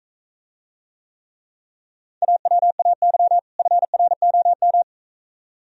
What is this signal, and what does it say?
Morse code, heard as a keyed tone, about 44 wpm, 700 Hz. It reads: AWAY FROM